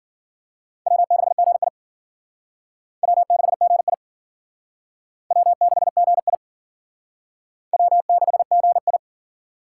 Morse code, heard as a keyed tone, about 52 wpm, 700 Hz. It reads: W6GI W6GI W6GI W6GI